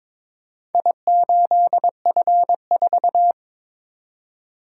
Morse code, heard as a keyed tone, about 23 wpm, 700 Hz. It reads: I8F4
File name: Four Letter-Number Combinations - 22wpm